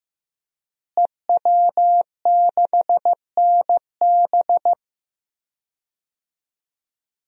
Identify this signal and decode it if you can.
Morse code, heard as a keyed tone, about 15 wpm, 700 Hz. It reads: EW6NB
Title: DXCC-100 Call Signs - 5 Characters - Repeated 3x - 15wpm